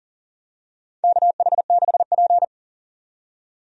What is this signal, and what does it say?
Morse code, heard as a keyed tone, about 40 wpm, 700 Hz. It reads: KH6P
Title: DXCC-100 Call Signs - 4 Characters - 40wpm